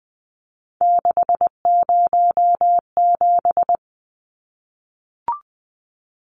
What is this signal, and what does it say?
Morse code, heard as a keyed tone, about 20 wpm, 700 Hz. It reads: 607
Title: Numbers - 3-Digit 20wpm